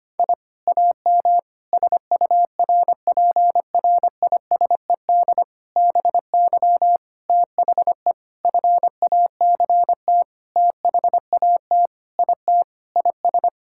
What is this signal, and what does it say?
Morse code, heard as a keyed tone, about 25 wpm, 700 Hz. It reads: I AM SURPRISED BY THE FACT THAT IT IS
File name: practice / 25wpm / Sentences with 4-Word Phrases - 25wpm